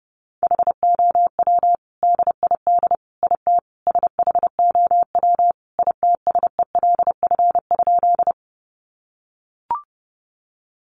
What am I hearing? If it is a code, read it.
Morse code, heard as a keyed tone, about 30 wpm, 700 Hz. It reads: HOW DID IT SHOW ITSELF?
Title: Sentences from Top 800 Words 30wpm